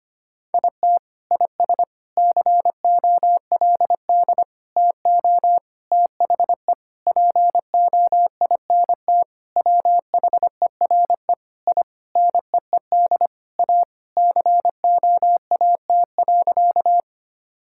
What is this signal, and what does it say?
Morse code, heard as a keyed tone, about 25 wpm, 700 Hz. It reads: IT IS COLD TO THE POINT WHERE I NEED A COAT.
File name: practice / 25wpm / Sentences with 4-Word Phrases - 25wpm